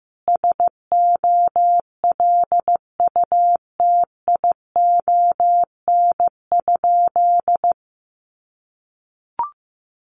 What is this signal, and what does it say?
Morse code, heard as a keyed tone, about 15 wpm, 700 Hz. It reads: SOLUTION?
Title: Sentences from Top 1300 Words 15wpm